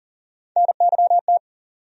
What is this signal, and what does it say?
Morse code, heard as a keyed tone, about 37 wpm, 700 Hz. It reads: NYT
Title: USA Call Sign Postfixes - 3 Letters 40wpm